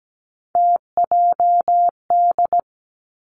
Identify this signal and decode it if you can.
Morse code, heard as a keyed tone, about 17 wpm, 700 Hz. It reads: TJD
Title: Three Challenging Letter-Number Combinations 17wpm